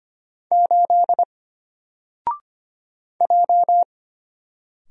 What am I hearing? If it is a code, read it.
Morse code, heard as a keyed tone, about 25 wpm, 700 Hz. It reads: 8 J
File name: Single-Letter-Number-25wpm